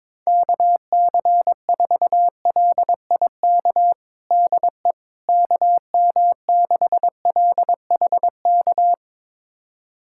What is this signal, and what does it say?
Morse code, heard as a keyed tone, about 22 wpm, 700 Hz. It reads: KC4LIK DE KM6LHK
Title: USA Callsign DE Callsign - 6 Characters - 22wpm